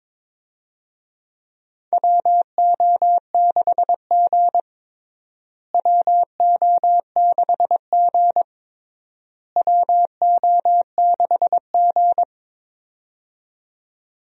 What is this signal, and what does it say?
Morse code, heard as a keyed tone, about 21 wpm, 700 Hz. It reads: WO6G WO6G WO6G
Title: USA Call Signs - 4 Characters - Repeated 3x - 22wpm